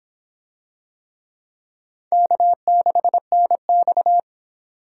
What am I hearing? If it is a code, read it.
Morse code, heard as a keyed tone, about 26 wpm, 700 Hz. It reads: K6NX